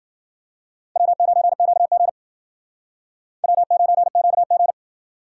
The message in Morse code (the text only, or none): W9QG W9QG